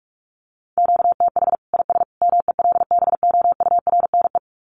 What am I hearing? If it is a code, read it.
Morse code, heard as a keyed tone, about 44 wpm, 700 Hz. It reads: QTH IS MELBOURNE
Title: Common QSO Phrases 45wpm